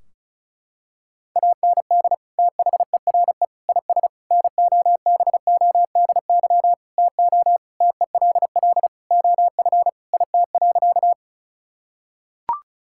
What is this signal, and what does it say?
Morse code, heard as a keyed tone, about 36 wpm, 700 Hz. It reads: AND THERE IS NOBODY TO TELL OF IT.
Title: Sentences from Top 1600 Words 35wpm